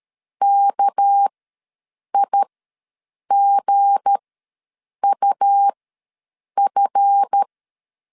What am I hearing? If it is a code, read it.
Morse code, heard as a keyed tone, about 13 wpm, spaced as for 7 wpm, 800 Hz. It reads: KIGUF
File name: LEZIONI / 7 GF9Z / 7 GF ws 12wpm 3140